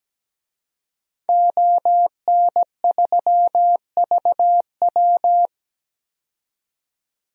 Morse code, heard as a keyed tone, about 17 wpm, 700 Hz. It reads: ON3VW
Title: DXCC-100 Call Signs - 5 Characters - 17wpm